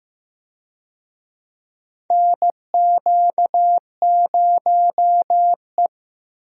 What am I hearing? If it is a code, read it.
Morse code, heard as a keyed tone, about 16 wpm, 700 Hz. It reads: NQ0E
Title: USA Call Signs - 4 Characters - 15wpm